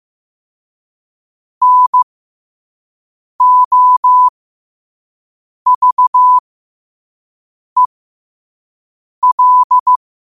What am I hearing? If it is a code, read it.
Morse code, heard as a keyed tone, about 15 wpm, spaced as for 5 wpm, 1000 Hz. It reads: NOVEL
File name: qotdfiles / 5wpm